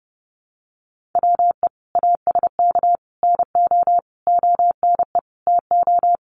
Morse code, heard as a keyed tone, about 30 wpm, 700 Hz. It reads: WE ASK NO ONE TO
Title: Sentences from Top 1500 Words 30wpm